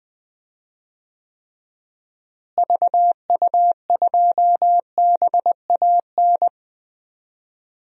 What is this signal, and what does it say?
Morse code, heard as a keyed tone, about 20 wpm, 700 Hz. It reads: VU2BAN
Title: DXCC-100 Call Signs - 6 Characters - Repeated 3x - 20wpm